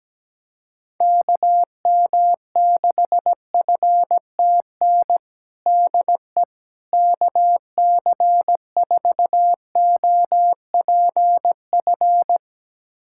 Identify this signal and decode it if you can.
Morse code, heard as a keyed tone, about 17 wpm, 700 Hz. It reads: KM6FTN DE KC4OPF
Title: USA Callsign DE Callsign 17wpm